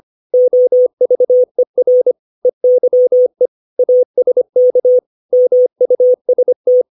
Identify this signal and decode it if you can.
Morse code, heard as a keyed tone, about 25 wpm, 500 Hz. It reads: OVER EYE ASK MUST
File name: 632 words / 632 Words 25WPM